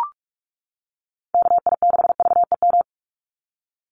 Morse code, heard as a keyed tone, about 44 wpm, 700 Hz. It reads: KI6VEM
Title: USA Call Signs 45wpm